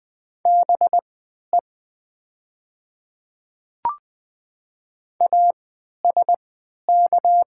Morse code, heard as a keyed tone, about 20 wpm, spaced as for 11 wpm, 700 Hz. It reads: BE ASK
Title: 3x Character Spacing - Top 200 Words - 20wpm